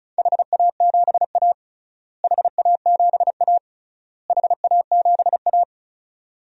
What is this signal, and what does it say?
Morse code, heard as a keyed tone, about 34 wpm, 700 Hz. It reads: HA7A HA7A HA7A
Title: DXCC-100 Call Signs - 4 Characters - Repeated 3x - 35wpm